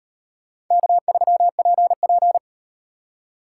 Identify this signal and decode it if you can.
Morse code, heard as a keyed tone, about 37 wpm, 700 Hz. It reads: K3PP